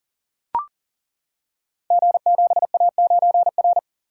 Morse code, heard as a keyed tone, about 40 wpm, 700 Hz. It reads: G7A9R
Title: Code Groups - 5 Letter:Numbers - 40wpm